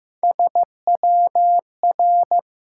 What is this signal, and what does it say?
Morse code, heard as a keyed tone, about 15 wpm, 700 Hz. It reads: SWR
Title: Top 180 QSO Elements 15wpm